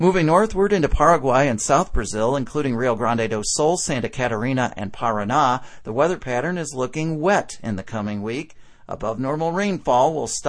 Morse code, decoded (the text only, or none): none